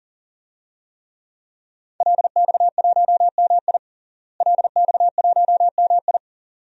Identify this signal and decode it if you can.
Morse code, heard as a keyed tone, about 40 wpm, 700 Hz. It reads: LX1MI LX1MI
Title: DXCC-100 Call Signs - 5 Characters - Repeated 2x - 40wpm